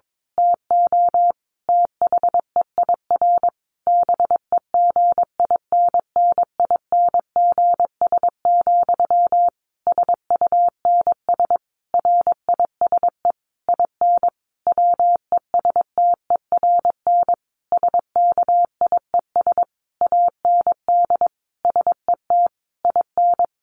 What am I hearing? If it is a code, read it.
Morse code, heard as a keyed tone, about 22 wpm, 700 Hz. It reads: TO THEIR BEGINNINGS, SUNS RISE IN WESTERN SKIES AND SET IN
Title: A Sound Of Thunder 22wpm